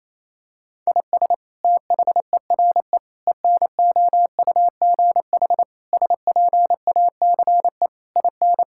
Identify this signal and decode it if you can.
Morse code, heard as a keyed tone, about 28 wpm, 700 Hz. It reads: IS THERE ENOUGH SPACE IN
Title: Sentences from Top 1000 Words - Review Part 5 - 28wpm